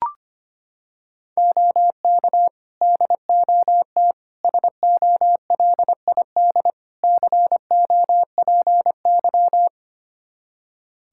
Morse code, heard as a keyed tone, about 25 wpm, 700 Hz. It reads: OK DOT SOLID COPY
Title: Practice QSO - Exchange 3 - More Info - 25wpm